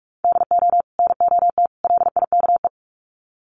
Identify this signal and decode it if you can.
Morse code, heard as a keyed tone, about 44 wpm, 700 Hz. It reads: DO NOT LIKE
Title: Sets of 3 Words 45wpm